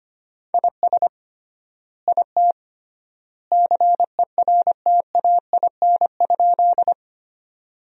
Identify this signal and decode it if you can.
Morse code, heard as a keyed tone, about 25 wpm, 700 Hz. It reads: IS IT CERTAIN?
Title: 3x Word-Spacing - Sentences from Top 500 Words - 25wpm